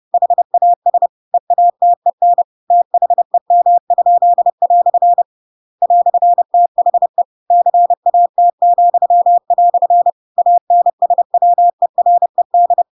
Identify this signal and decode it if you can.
Morse code, heard as a keyed tone, about 30 wpm, 700 Hz. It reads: HAS EATEN THEM?" "THE CAT," ANSWERED
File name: Pinocchio / Pinocchio ch7 30 30 ik1hge